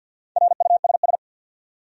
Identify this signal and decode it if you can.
Morse code, heard as a keyed tone, about 52 wpm, 700 Hz. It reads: RUSS
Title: USA Common Names - Max 4 Letters - 50wpm